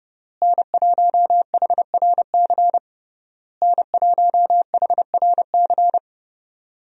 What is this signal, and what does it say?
Morse code, heard as a keyed tone, about 32 wpm, 700 Hz. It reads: N1HRC N1HRC
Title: USA Call Signs - 5 Characters - Repeated 3x - 30wpm